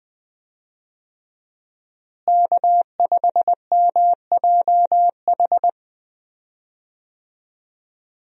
Morse code, heard as a keyed tone, about 20 wpm, 700 Hz. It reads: K5MJH
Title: USA Call Signs - 5 Characters - Repeated 2x - 20wpm